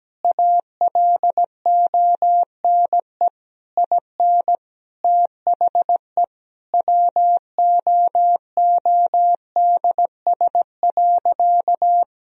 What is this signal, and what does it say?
Morse code, heard as a keyed tone, about 17 wpm, 700 Hz. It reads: ALONE IN THE WOODS.
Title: Sentences from Top 1000 Words - Review Part 2 - 17wpm